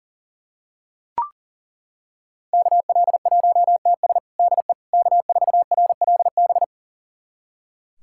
Morse code, heard as a keyed tone, about 40 wpm, 700 Hz. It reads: KL1TS DE K4RLB
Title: USA Callsign DE Callsign 40wpm